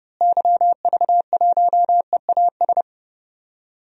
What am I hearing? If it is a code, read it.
Morse code, heard as a keyed tone, about 30 wpm, 700 Hz. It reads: YV1EAS
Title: DXCC-100 Call Signs - 6 Characters - Repeated 2x - 30wpm